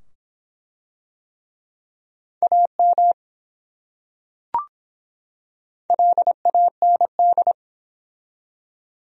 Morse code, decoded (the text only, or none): AM LAND